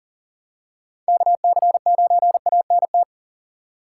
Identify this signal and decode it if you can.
Morse code, heard as a keyed tone, about 40 wpm, 700 Hz. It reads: KC9ANT
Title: USA Call Signs - 6x Character Spacing - 40wpm